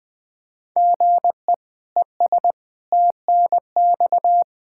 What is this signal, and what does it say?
Morse code, heard as a keyed tone, about 20 wpm, 700 Hz. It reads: GE ES TNX
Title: Practice QSO - Exchange 1 - Beginning - 20wpm